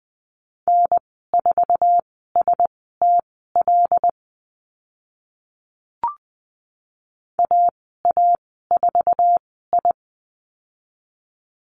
Morse code, heard as a keyed tone, about 20 wpm, spaced as for 14 wpm, 700 Hz. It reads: N4STL AA4I
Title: 082 Lesson 14g - Number 2 - Call Signs 20wpm